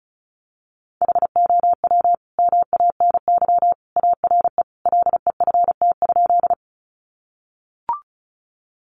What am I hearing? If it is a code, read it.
Morse code, heard as a keyed tone, about 34 wpm, 700 Hz. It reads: HOW MANY ARE LEFT?
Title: Sentences from Top 300 Words 35wpm